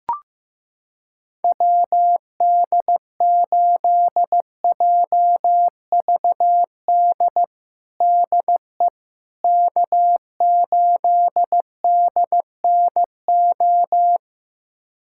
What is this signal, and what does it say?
Morse code, heard as a keyed tone, about 16 wpm, 700 Hz. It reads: WD8JVD DE K8DNO
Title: USA Callsign DE Callsign 15wpm